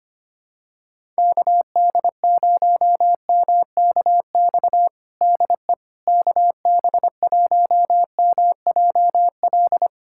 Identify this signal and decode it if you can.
Morse code, heard as a keyed tone, about 25 wpm, 700 Hz. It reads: KD0MKX DE KB1MJL
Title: USA Callsign DE Callsign - 6 Characters - 25wpm